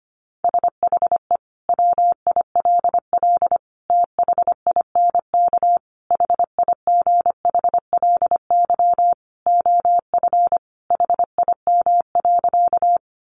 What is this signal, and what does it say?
Morse code, heard as a keyed tone, about 25 wpm, 700 Hz. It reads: SHE WILL THINK HIGHLY OF HIM.
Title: Sentences from Top 1100 Words 25wpm